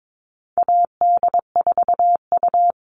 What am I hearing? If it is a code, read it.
Morse code, heard as a keyed tone, about 22 wpm, 700 Hz. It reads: AD4U
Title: All Cut Numbers - 4-Digit 22wpm